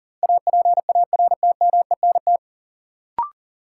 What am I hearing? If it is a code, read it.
Morse code, heard as a keyed tone, about 38 wpm, 700 Hz. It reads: APARTMENT
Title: Sentences from Top 900 Words - New Word Drill - 40wpm